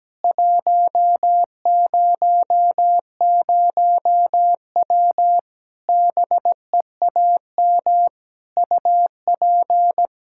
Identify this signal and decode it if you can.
Morse code, heard as a keyed tone, about 17 wpm, 700 Hz. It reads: 100W BEAM UP
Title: Practice QSO - Exchange 2 - Info - 17wpm